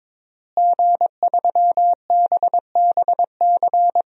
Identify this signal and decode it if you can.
Morse code, heard as a keyed tone, about 22 wpm, 700 Hz. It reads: G3BBC
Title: DXCC-100 Call Signs - 5 Characters - Repeated 3x - 22wpm